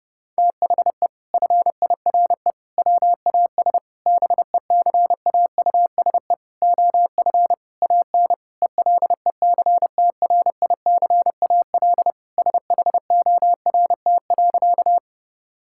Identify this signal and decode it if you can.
Morse code, heard as a keyed tone, about 30 wpm, 700 Hz. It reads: THE FIRE WAS BECAUSE OF AN ELECTRICAL SHORT.
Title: Sentences from Top 1000 Words - Review Part 6 - 30wpm